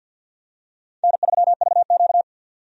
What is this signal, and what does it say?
Morse code, heard as a keyed tone, about 51 wpm, 700 Hz. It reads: N3VQ
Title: USA Call Signs - 4 Characters - 50wpm